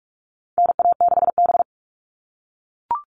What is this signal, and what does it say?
Morse code, heard as a keyed tone, about 44 wpm, 700 Hz. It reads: NA6B
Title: All Cut Numbers - 4-Digit 45wpm